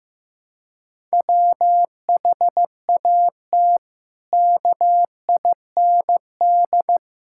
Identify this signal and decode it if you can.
Morse code, heard as a keyed tone, about 15 wpm, 700 Hz. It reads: WHAT KIND
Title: Sentences from Top 1000 Words - Review Part 5 - 15wpm